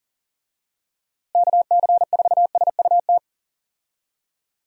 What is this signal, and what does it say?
Morse code, heard as a keyed tone, about 40 wpm, 700 Hz. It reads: KC4SUT
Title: USA Call Signs - 6 Characters - Repeated 2x - 40wpm